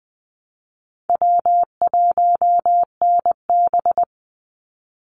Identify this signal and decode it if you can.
Morse code, heard as a keyed tone, about 20 wpm, 700 Hz. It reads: W1NB
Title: USA Call Signs 20wpm